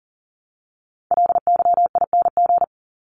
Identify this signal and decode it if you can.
Morse code, heard as a keyed tone, about 40 wpm, 700 Hz. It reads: LYING
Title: ING Suffix - Top 100 Words with Base Words 40wpm